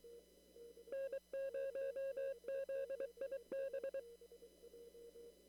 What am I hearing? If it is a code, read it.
Morse code, heard as a keyed tone, about 23 wpm, 540 Hz. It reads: N0ZIB
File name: CW / kt5j wpx cw-op-k5pi